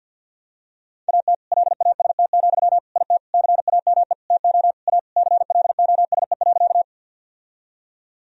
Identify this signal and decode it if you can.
Morse code, heard as a keyed tone, about 48 wpm, 700 Hz. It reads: AT LAST, IT CAME TO A CLOSE.
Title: Sentences from Top 1000 Words - Review Part 1 - 50wpm